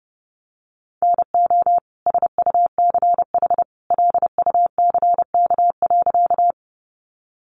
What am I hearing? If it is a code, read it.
Morse code, heard as a keyed tone, about 30 wpm, 700 Hz. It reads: NO SUCH LUCK.
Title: Sentences from Top 1500 Words 30wpm